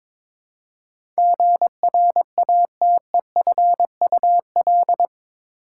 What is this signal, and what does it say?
Morse code, heard as a keyed tone, about 22 wpm, 700 Hz. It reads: GRATEFUL
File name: practice / 22wpm / Eight Letter Words - 22wpm